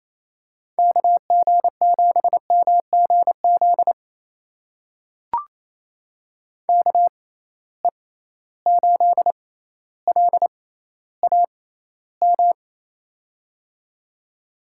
Morse code, heard as a keyed tone, about 27 wpm, spaced as for 14 wpm, 700 Hz. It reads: KG7MGZ KE8LAM
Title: USA Call Signs - 6 Characters - 6x Character Spacing - 28wpm